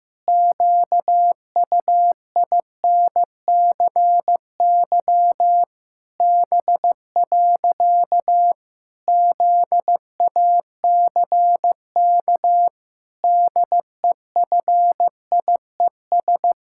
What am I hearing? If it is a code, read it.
Morse code, heard as a keyed tone, about 15 wpm, 700 Hz. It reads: QUINCY B. ZACK DEFIES
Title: Pangram Challenge v2 - 15wpm